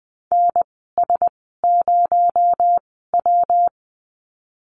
Morse code, heard as a keyed tone, about 20 wpm, spaced as for 14 wpm, 700 Hz. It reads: NS0W
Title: 219 Lesson 35g - Number 0 - Call Signs 20wpm